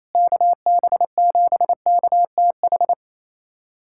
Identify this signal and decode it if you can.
Morse code, heard as a keyed tone, about 28 wpm, 700 Hz. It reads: KB7KTH
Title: USA Callsign DE Callsign 28wpm